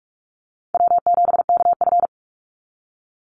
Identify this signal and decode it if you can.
Morse code, heard as a keyed tone, about 44 wpm, 700 Hz. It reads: W7KF
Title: Easy Call Signs Repeated 3x - 45wpm